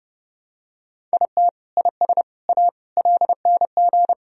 Morse code, heard as a keyed tone, about 30 wpm, 700 Hz. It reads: IT IS A LNG
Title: Sentences from Top 300 Words and Abbreviations 30wpm